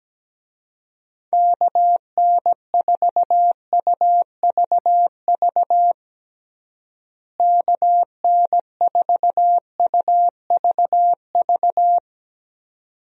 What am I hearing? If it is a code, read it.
Morse code, heard as a keyed tone, about 17 wpm, 700 Hz. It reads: KN4UVV KN4UVV
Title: USA Call Signs - 6 Characters - Repeated 2x - 17wpm